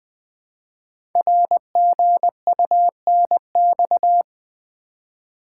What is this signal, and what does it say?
Morse code, heard as a keyed tone, about 20 wpm, 700 Hz. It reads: RGUNX
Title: Code Groups - 5 Letters - 20wpm